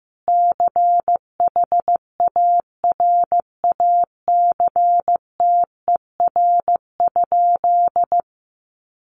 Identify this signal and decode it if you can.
Morse code, heard as a keyed tone, about 15 wpm, 700 Hz. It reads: CHARACTER?
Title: Sentences from Top 1200 Words 15wpm